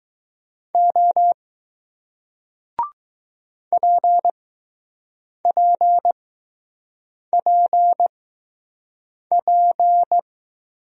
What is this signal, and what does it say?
Morse code, heard as a keyed tone, about 20 wpm, spaced as for 6 wpm, 700 Hz. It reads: O PPPP